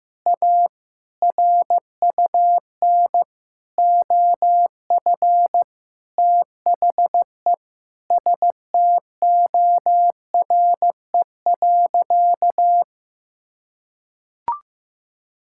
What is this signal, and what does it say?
Morse code, heard as a keyed tone, about 15 wpm, 700 Hz. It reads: A RUN OF THE STORE.
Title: Sentences from Top 1000 Words - Review Part 5 - 15wpm